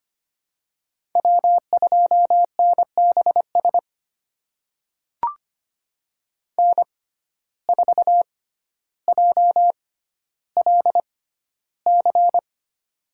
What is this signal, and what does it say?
Morse code, heard as a keyed tone, about 25 wpm, spaced as for 13 wpm, 700 Hz. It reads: W2NBS N4JLC
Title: USA Call Signs - 6x Character Spacing - 25wpm